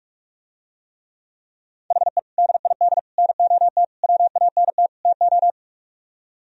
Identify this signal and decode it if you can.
Morse code, heard as a keyed tone, about 49 wpm, 700 Hz. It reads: HE DID NOT WANT TO